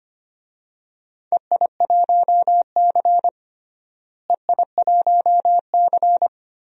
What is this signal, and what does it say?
Morse code, heard as a keyed tone, about 25 wpm, 700 Hz. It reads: EI1C EI1C
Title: DXCC-100 Call Signs - 4 Characters - Repeated 2x - 25wpm